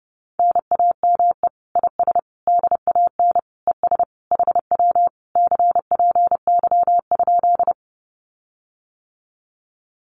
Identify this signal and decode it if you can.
Morse code, heard as a keyed tone, about 30 wpm, 700 Hz. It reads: NAME IS DAN ES HW CPY?
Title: Common QSO Phrases Encore 30wpm